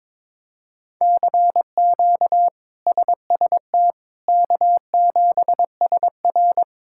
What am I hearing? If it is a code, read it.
Morse code, heard as a keyed tone, about 22 wpm, 700 Hz. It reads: CQ SST K7SR
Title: K1USN SST QSO Practice - 22wpm